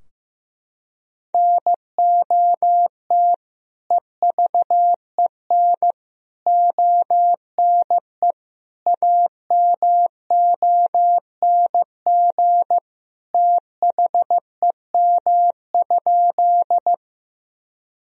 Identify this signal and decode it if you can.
Morse code, heard as a keyed tone, about 15 wpm, 700 Hz. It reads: NOT EVEN ONE AMONG THEM?
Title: Sentences from Top 1000 Words - Review Part 3 - 15wpm